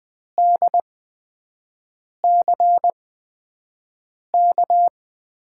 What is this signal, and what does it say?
Morse code, heard as a keyed tone, about 20 wpm, spaced as for 5 wpm, 700 Hz. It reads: DCK
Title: USA Call Signs - 6 Characters - 8x Character Spacing - 20wpm